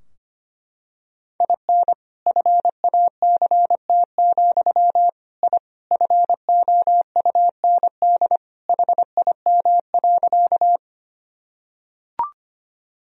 Morse code, heard as a keyed tone, about 25 wpm, 700 Hz. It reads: IN FACT, I FOUND HIM.
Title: Sentences from Top 1000 Words - Review Part 3 - 25wpm